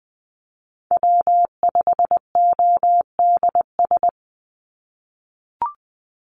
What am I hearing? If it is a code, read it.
Morse code, heard as a keyed tone, about 20 wpm, 700 Hz. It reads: W5ODS
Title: NA SOTA Chaser Callsigns 20wpm